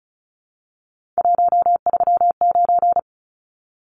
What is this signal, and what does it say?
Morse code, heard as a keyed tone, about 34 wpm, 700 Hz. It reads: 139
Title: Numbers - 3-Digit 35wpm